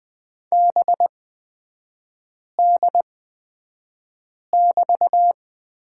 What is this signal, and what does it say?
Morse code, heard as a keyed tone, about 20 wpm, spaced as for 5 wpm, 700 Hz. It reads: BD<BT>
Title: 224 Lesson 36e - BT - Three Characters 20wpm